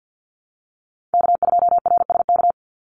Morse code, heard as a keyed tone, about 50 wpm, 700 Hz. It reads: K2RSK
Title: Call Signs - 50wpm